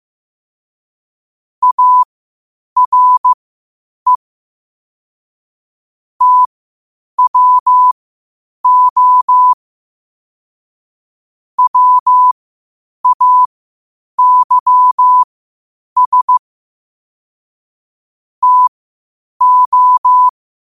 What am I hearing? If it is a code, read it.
Morse code, heard as a keyed tone, about 15 wpm, spaced as for 8 wpm, 1000 Hz. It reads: ARE TWO WAYS TO
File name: qotdfiles / 8wpm